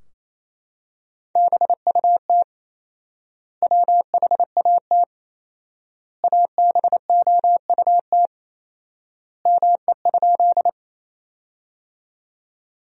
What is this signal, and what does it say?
Morse code, heard as a keyed tone, about 28 wpm, 700 Hz. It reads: BUT WHAT ABOUT ME?
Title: 4x Word-Spacing - Sentences from Top 200 Words - 28wpm